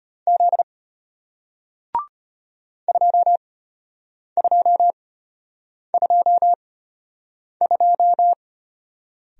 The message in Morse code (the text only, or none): Z 2222